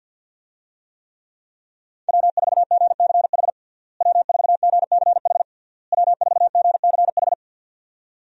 Morse code, heard as a keyed tone, about 49 wpm, 700 Hz. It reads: W4GCH W4GCH W4GCH
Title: USA Call Signs - 5 Characters - Repeated 3x - 50wpm